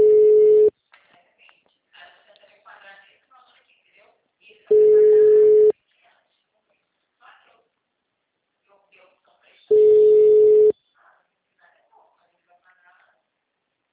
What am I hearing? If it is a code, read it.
Ringback tone (Europe, 425 Hz), with speech.
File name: out- -2012- - - .21539